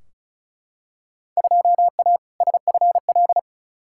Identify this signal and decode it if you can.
Morse code, heard as a keyed tone, about 36 wpm, 690 Hz. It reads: 2A SFL
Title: Field Day Exchanges Encore - 35wpm